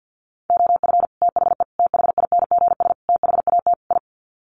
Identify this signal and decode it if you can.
Morse code, heard as a keyed tone, about 50 wpm, 700 Hz. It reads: OF THE THINGS THAT I
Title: Sets of 5 Words Encore 50wpm